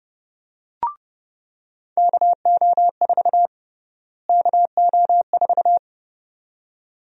Morse code, heard as a keyed tone, about 32 wpm, 700 Hz. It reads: KO4 KO4
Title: USA Call Sign Prefixes - Repeated 2x - 30wpm